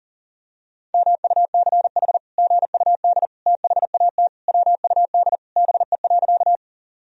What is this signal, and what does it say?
Morse code, heard as a keyed tone, about 40 wpm, 700 Hz. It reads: MUCH GUD THAT WUD BE.
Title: Sentences from Top 200 Words and Abbreviations 40wpm